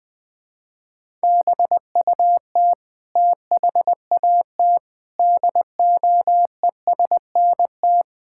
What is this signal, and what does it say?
Morse code, heard as a keyed tone, about 20 wpm, 700 Hz. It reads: BUT THAT DOESNT
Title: Sentences with 7-Word Phrases - Spoken Beginning - 20wpm